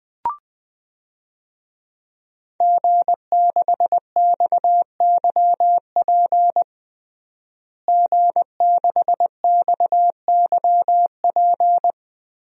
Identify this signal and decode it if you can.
Morse code, heard as a keyed tone, about 20 wpm, 700 Hz. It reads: G6XYP G6XYP
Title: DXCC-100 Call Signs - 5 Characters - Repeated 3x - 20wpm